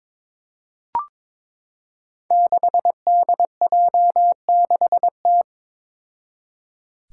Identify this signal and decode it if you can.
Morse code, heard as a keyed tone, about 22 wpm, 700 Hz. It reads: 6DJ6T
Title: Five Challenging Letter-Number Combinations 22wpm